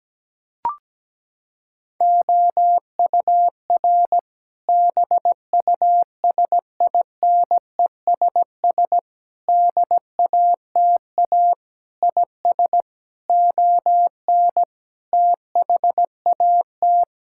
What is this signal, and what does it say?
Morse code, heard as a keyed tone, about 17 wpm, 700 Hz. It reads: OUR BUSINESS DATA IS ON THAT
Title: Sentences from Top 1000 Words - Review Part 6 - 17wpm